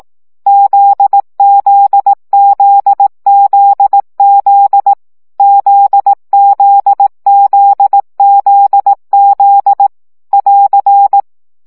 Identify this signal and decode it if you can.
Morse code, heard as a keyed tone, about 18 wpm, 800 Hz. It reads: ZZZZZ ZZZZZ <AR>
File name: CW gyakorlas z 15 baud